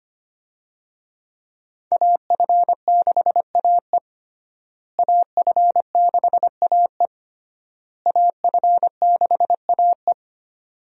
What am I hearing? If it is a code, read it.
Morse code, heard as a keyed tone, about 25 wpm, 700 Hz. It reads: AF6AE AF6AE AF6AE
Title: USA Call Signs - 5 Characters - Repeated 3x - 25wpm